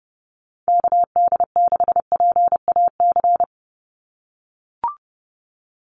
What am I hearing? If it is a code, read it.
Morse code, heard as a keyed tone, about 30 wpm, 700 Hz. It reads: KD6PAC
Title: USA Call Signs 30wpm